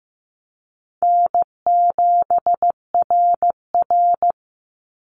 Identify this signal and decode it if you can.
Morse code, heard as a keyed tone, about 15 wpm, 700 Hz. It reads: N7RR
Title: NA SOTA Activator Callsigns 15wpm